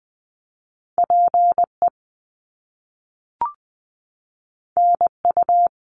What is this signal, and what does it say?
Morse code, heard as a keyed tone, about 20 wpm, 700 Hz. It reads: PE NU
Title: Canadian Provinces and Territories 20wpm